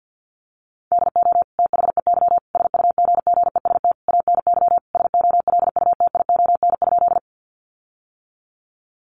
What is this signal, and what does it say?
Morse code, heard as a keyed tone, about 50 wpm, 700 Hz. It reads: DO THEY SUGGEST ANY SOLUTION?
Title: Sentences from Top 1300 Words 50wpm